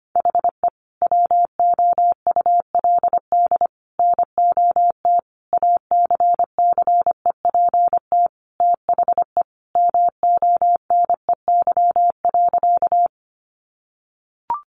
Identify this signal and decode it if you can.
Morse code, heard as a keyed tone, about 25 wpm, 700 Hz. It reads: HE WOULD NOT ACCEPT THE MONEY.
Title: Sentences from Top 1300 Words 25wpm